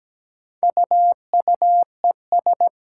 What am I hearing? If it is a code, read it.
Morse code, heard as a keyed tone, about 17 wpm, 700 Hz. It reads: UUES
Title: Four Challenging Letter-Number Combinations 17wpm